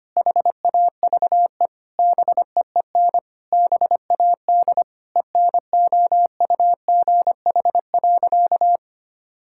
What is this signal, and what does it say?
Morse code, heard as a keyed tone, about 25 wpm, 700 Hz. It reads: HAVE BEEN BAD ENOUGH.
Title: Sentences from Top 1000 Words - Review Part 5 - 25wpm